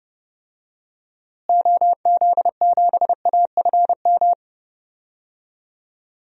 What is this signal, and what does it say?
Morse code, heard as a keyed tone, about 32 wpm, 700 Hz. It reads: OZ7AFM
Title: DXCC-100 Call Signs - 6 Characters - Repeated 3x - 30wpm